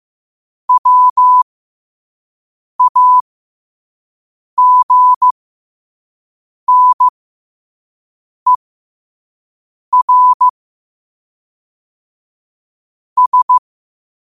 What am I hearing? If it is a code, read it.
Morse code, heard as a keyed tone, about 15 wpm, spaced as for 5 wpm, 1000 Hz. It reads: WAGNER S
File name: qotdfiles / 5wpm